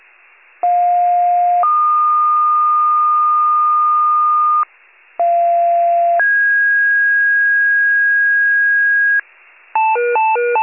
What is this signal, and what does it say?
Two-tone page, heard as a one-second tone, then a three-second tone, about 710 then 1220 Hz, about 690 then 1650 Hz, with noise. It is heard from 0.6 to 9.2 s.